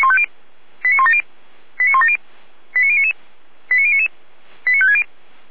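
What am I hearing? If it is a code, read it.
Five-tone selective call (ZVEI), heard as quick runs of five tones, tones of about 70 ms. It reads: 89170 89170 89099 89099 89560
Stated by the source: ZVEI five-tone selective call received on an SDR.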